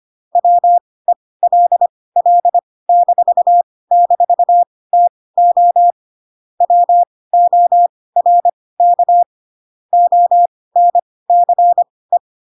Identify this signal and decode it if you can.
Morse code, heard as a keyed tone, about 25 wpm, spaced as for 18 wpm, 700 Hz. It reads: WELL--TO WORK ONCE
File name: Pinocchio / Pinocchio ch1 12 25 ik1hge